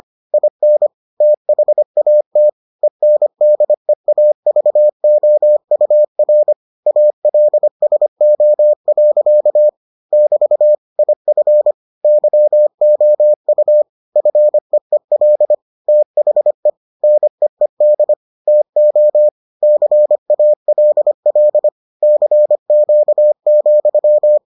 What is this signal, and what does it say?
Morse code, heard as a keyed tone, about 25 wpm, 600 Hz. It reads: IN THAT ENDEAVOUR ALSO. <BT> IF YOU FEEL THE NEED TO CALL CQ,